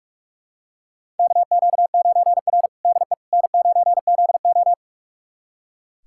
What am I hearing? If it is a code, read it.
Morse code, heard as a keyed tone, about 46 wpm, 700 Hz. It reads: KQ9R DE N9ZO
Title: USA Callsign DE Callsign - 4 Characters - 45wpm